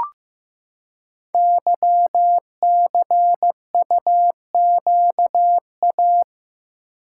Morse code, heard as a keyed tone, about 15 wpm, 700 Hz. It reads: YCUQA
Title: Code Groups - 5 Letters - 15wpm